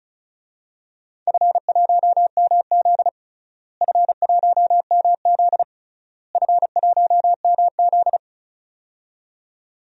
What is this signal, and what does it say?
Morse code, heard as a keyed tone, about 36 wpm, 700 Hz. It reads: F1MZ F1MZ F1MZ
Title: DXCC-100 Call Signs - 4 Characters - Repeated 3x - 35wpm